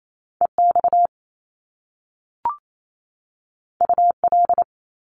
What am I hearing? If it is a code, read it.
Morse code, heard as a keyed tone, about 28 wpm, 700 Hz. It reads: EX UL
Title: Top Two Letter Combinations in Words 28wpm